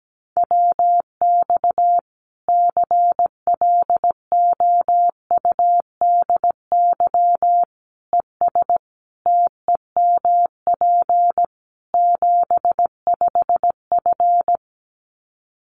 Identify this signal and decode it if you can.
Morse code, heard as a keyed tone, about 17 wpm, 700 Hz. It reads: WX CLOUDY ES TEMP 75F
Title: Common QSO Phrases 17wpm